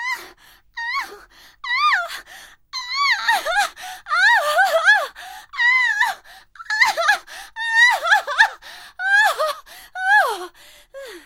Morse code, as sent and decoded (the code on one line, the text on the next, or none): none